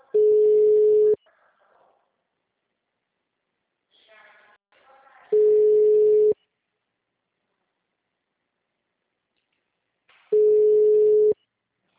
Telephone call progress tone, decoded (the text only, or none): ringback tone